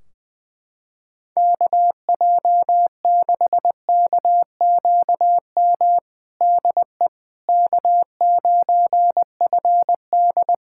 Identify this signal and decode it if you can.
Morse code, heard as a keyed tone, about 21 wpm, 700 Hz. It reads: KJ6KQM DE K9FD
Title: K1USN SST QSO Practice 20wpm